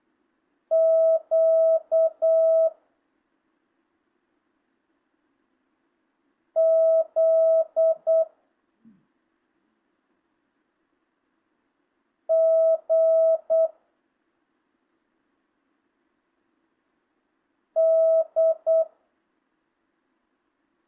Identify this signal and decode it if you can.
Morse code, heard as a keyed tone, about 8 wpm, 660 Hz. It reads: Q Z G D
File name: jh8cbh / 2025cw